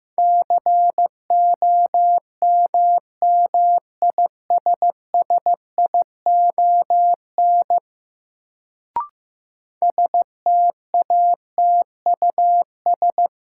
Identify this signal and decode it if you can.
Morse code, heard as a keyed tone, about 15 wpm, 700 Hz. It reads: COMMISSION STATUS
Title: Sentences from Top 1400 Words - New Word Drill Alt - 15wpm